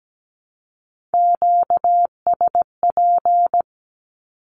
Morse code, heard as a keyed tone, about 17 wpm, 700 Hz. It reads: QSP
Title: Q-Codes 17wpm